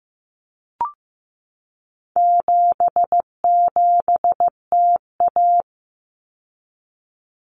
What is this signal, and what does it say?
Morse code, heard as a keyed tone, about 15 wpm, 700 Hz. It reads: 77TA
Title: Common Cut Numbers - 4-Digit 15wpm